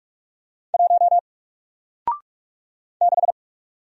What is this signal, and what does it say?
Morse code, heard as a keyed tone, about 44 wpm, 700 Hz. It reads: 1 6
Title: Single Letter-Number 45wpm